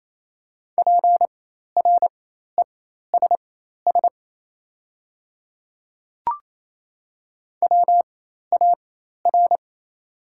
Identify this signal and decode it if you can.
Morse code, heard as a keyed tone, about 28 wpm, spaced as for 13 wpm, 700 Hz. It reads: PRESS WAR
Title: 4x Character Spacing - Top 500 Words - 28wpm